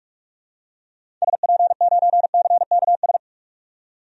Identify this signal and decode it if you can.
Morse code, heard as a keyed tone, about 46 wpm, 700 Hz. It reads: SP9CKS
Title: DXCC-100 Call Signs - 6 Characters - 45wpm